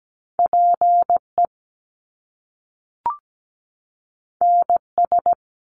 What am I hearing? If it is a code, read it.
Morse code, heard as a keyed tone, about 17 wpm, 700 Hz. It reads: PE NS
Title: Canadian Provinces and Territories 17wpm